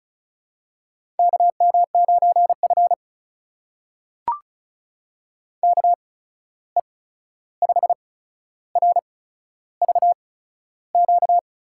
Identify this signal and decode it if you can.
Morse code, heard as a keyed tone, about 36 wpm, spaced as for 13 wpm, 700 Hz. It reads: KM9F KE5RVQ
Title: USA Call Signs - 8x Character Spacing - 35wpm